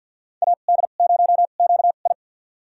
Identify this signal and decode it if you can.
Morse code, heard as a keyed tone, about 48 wpm, spaced as for 35 wpm, 700 Hz. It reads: AD0QI
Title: USA Call Signs - 2x Character Spacing - 50wpm